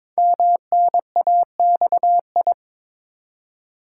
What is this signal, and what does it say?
Morse code, heard as a keyed tone, about 23 wpm, 700 Hz. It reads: MNAXI
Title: Code Groups - 5 Letters - 22wpm